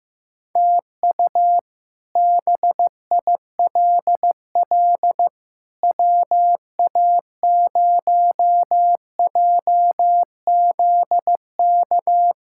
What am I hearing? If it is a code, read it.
Morse code, heard as a keyed tone, about 15 wpm, 700 Hz. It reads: TU BILL WA0JZK
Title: K1USN SST QSO Practice - 15wpm